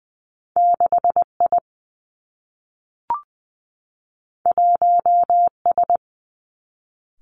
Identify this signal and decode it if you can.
Morse code, heard as a keyed tone, about 20 wpm, 700 Hz. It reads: 6 I 1 S
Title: Two Challenging Letter-Number Combinations 20wpm